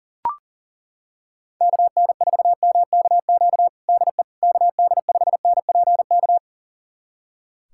Morse code, heard as a keyed tone, about 40 wpm, 700 Hz. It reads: KN4MKQ DE KD5NPK
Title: USA Callsign DE Callsign - 6 Characters - 40wpm